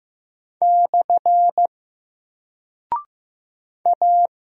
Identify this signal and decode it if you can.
Morse code, heard as a keyed tone, about 15 wpm, 700 Hz. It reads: / A
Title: Single Character (Letter:Number:Punctuation:Prosign) 15wpm